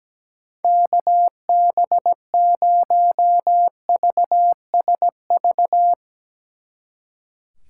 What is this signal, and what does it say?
Morse code, heard as a keyed tone, about 18 wpm, 700 Hz. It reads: KB0VSV
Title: Speed Racing - USA Call Signs 17wpm